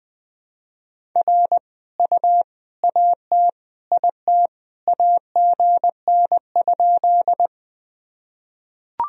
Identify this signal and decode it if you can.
Morse code, heard as a keyed tone, about 20 wpm, 700 Hz. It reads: R U AT IT AGN?
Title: Sentences from Top 200 Words and Abbreviations 20wpm